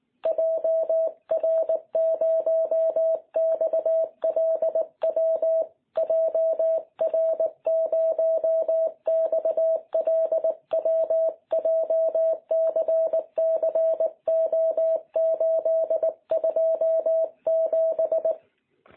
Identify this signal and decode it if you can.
Morse code, heard as a keyed tone, about 19 wpm, 650 Hz. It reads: JR0XLW JR0XLW JCCO827
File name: jh8cbh / 2021cw